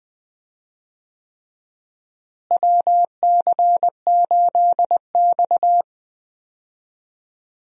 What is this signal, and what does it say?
Morse code, heard as a keyed tone, about 20 wpm, 700 Hz. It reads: WC8X
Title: USA Call Signs - 4 Characters - Repeated 3x - 20wpm